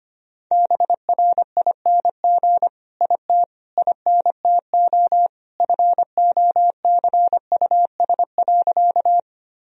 Morse code, heard as a keyed tone, about 25 wpm, 700 Hz. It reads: BRING IT INTO FOCUS.
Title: Sentences from Top 1000 Words - Review Part 4 - 25wpm